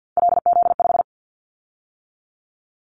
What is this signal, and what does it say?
Morse code, heard as a keyed tone, about 50 wpm, 700 Hz. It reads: LZ5
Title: Callsign Prefixes for Top 25 DXCC - 50wpm